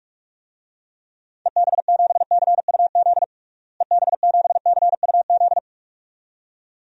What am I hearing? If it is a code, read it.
Morse code, heard as a keyed tone, about 44 wpm, 700 Hz. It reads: EB7CUZ EB7CUZ
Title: DXCC-100 Call Signs - 6 Characters - Repeated 2x - 45wpm